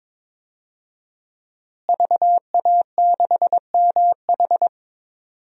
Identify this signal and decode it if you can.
Morse code, heard as a keyed tone, about 22 wpm, 700 Hz. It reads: VA6MH
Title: DXCC-100 Call Signs - 5 Characters - Repeated 3x - 22wpm